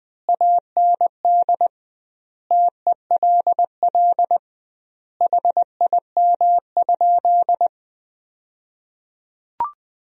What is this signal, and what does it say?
Morse code, heard as a keyed tone, about 20 wpm, 700 Hz. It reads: AND TELL HIM?
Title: 2x Word-Spacing - Sentences from Top 200 Words - 20wpm